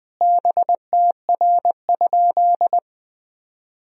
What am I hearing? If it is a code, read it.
Morse code, heard as a keyed tone, about 20 wpm, 700 Hz. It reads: BTR?
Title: Sentences from Top 400 Words and Abbreviations 20wpm